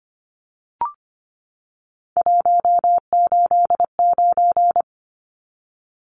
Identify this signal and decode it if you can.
Morse code, heard as a keyed tone, about 25 wpm, 700 Hz. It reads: 189
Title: Numbers - 3-Digit 25wpm